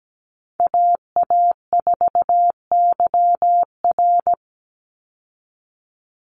Morse code, heard as a keyed tone, about 17 wpm, 700 Hz. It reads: AA4YR
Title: USA Call Signs 17wpm